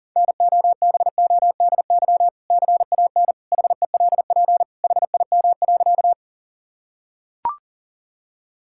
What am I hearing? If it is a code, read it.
Morse code, heard as a keyed tone, about 40 wpm, 700 Hz. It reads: NOBODY CAN HELP HIM.
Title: Sentences from Top 1600 Words 40wpm